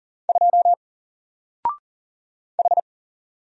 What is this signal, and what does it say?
Morse code, heard as a keyed tone, about 38 wpm, 700 Hz. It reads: 2 H
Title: Single Character (Letter:Number:Punctuation:Prosign) 40wpm